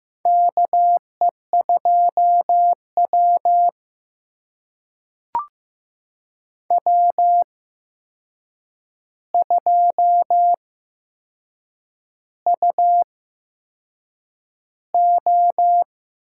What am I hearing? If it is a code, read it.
Morse code, heard as a keyed tone, about 15 wpm, spaced as for 6 wpm, 700 Hz. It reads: KE2W W2UO
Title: USA Call Signs - 4 Characters - 8x Character Spacing - 15wpm